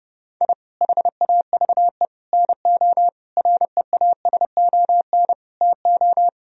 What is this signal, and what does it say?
Morse code, heard as a keyed tone, about 30 wpm, 700 Hz. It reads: I HAVE NO REASON TO
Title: Sentences from Top 1000 Words - Review Part 1 - 30wpm